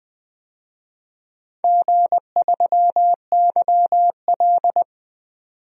Morse code, heard as a keyed tone, about 20 wpm, 700 Hz. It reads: G3YL
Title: DXCC-100 Call Signs - 4 Characters - 20wpm